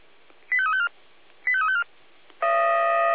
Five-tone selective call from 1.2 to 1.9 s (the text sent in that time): 85354